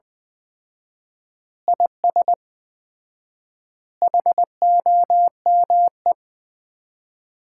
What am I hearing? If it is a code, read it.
Morse code, heard as a keyed tone, about 20 wpm, 700 Hz. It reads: IS HOME
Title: Common QSO Phrases - 4x Word-Spacing - 20wpm